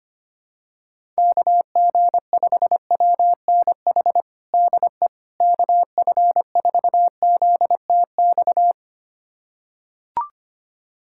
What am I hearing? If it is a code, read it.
Morse code, heard as a keyed tone, about 25 wpm, 700 Hz. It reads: KG5WNH DE KF4ZTX
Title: USA Callsign DE Callsign 25wpm